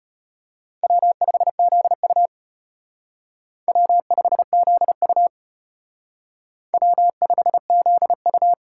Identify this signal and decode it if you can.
Morse code, heard as a keyed tone, about 33 wpm, 700 Hz. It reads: W5ZU W5ZU W5ZU